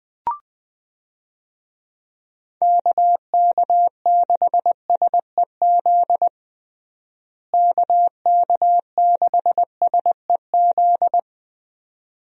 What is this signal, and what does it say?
Morse code, heard as a keyed tone, about 20 wpm, 700 Hz. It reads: KK6SEZ KK6SEZ
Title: USA Call Signs - 6 Characters - Repeated 2x - 20wpm